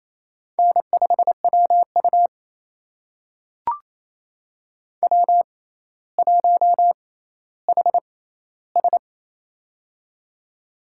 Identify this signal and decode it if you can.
Morse code, heard as a keyed tone, about 28 wpm, spaced as for 14 wpm, 700 Hz. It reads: N5WU W1HS
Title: USA Call Signs - 6x Character Spacing - 28wpm